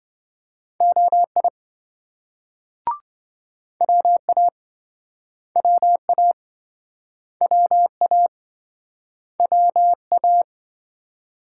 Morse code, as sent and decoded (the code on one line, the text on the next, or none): --- .. / .-- .- / .-- .- / .-- .- / .-- .-
OI WA WA WA WA